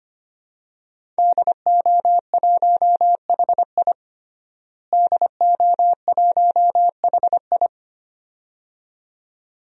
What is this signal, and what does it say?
Morse code, heard as a keyed tone, about 25 wpm, 700 Hz. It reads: DO1HI DO1HI
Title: DXCC-100 Call Signs - 5 Characters - Repeated 2x - 25wpm